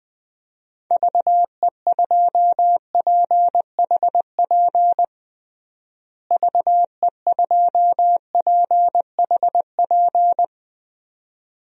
Morse code, heard as a keyed tone, about 20 wpm, 700 Hz. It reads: VE2PHP VE2PHP
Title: DXCC-100 Call Signs - 6 Characters - Repeated 2x - 20wpm